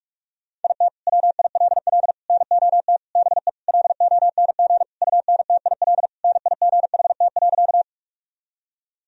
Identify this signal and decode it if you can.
Morse code, heard as a keyed tone, about 46 wpm, 700 Hz. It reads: IT WILL NOT BE LONG UNTIL NIGHT.
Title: Sentences from Top 1000 Words - Review Part 1 - 45wpm